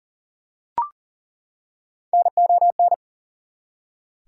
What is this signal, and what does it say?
Morse code, heard as a keyed tone, about 40 wpm, 700 Hz. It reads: NON
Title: Prefixes and Suffixes 40wpm